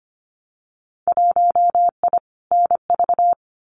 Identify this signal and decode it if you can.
Morse code, heard as a keyed tone, about 25 wpm, 700 Hz. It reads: 1I NV
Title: Winter Field Day Exchanges Encore 25wpm